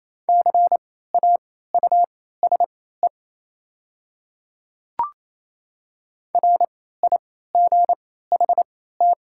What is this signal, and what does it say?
Morse code, heard as a keyed tone, about 28 wpm, spaced as for 16 wpm, 700 Hz. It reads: CAUSE RIGHT
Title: 3x Character Spacing - Top 200 Words - 28wpm